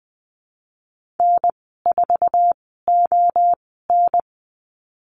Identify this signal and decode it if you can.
Morse code, heard as a keyed tone, about 20 wpm, spaced as for 15 wpm, 700 Hz. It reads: N4ON
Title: 076 Lesson 13g - Letter L - Call Signs 20wpm